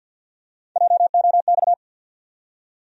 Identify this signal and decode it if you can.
Morse code, heard as a keyed tone, about 50 wpm, 700 Hz. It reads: JOX
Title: USA Call Sign Postfixes - 3 Letters 50wpm